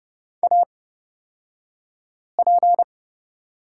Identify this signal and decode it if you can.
Morse code, heard as a keyed tone, about 30 wpm, spaced as for 5 wpm, 700 Hz. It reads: AP
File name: Lesson 37d - Letter x - Two Characters 30wpm